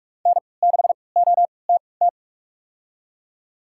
Morse code, heard as a keyed tone, about 41 wpm, spaced as for 25 wpm, 700 Hz. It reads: N6OTT
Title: USA Call Signs - 5 Characters - 3x Character Spacing - 45wpm